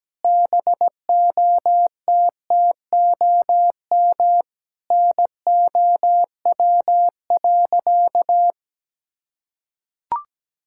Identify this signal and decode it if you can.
Morse code, heard as a keyed tone, about 17 wpm, 700 Hz. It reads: BOTTOM NOW.
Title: Sentences from Top 1000 Words - Review Part 6 - 17wpm